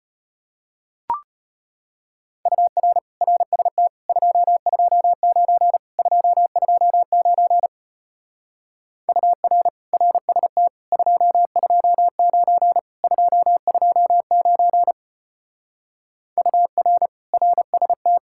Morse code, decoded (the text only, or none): UR RST 229 229 UR RST 229 229 UR RST